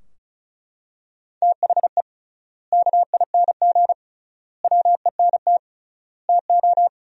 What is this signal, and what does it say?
Morse code, heard as a keyed tone, about 36 wpm, 700 Hz. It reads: THE KING WENT TO
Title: 3x Word-Spacing - Sentences from Top 400 Words - 35wpm